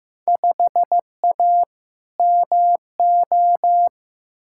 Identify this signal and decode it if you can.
Morse code, heard as a keyed tone, about 15 wpm, 700 Hz. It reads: 5A MO
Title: Field Day Exchanges - 15wpm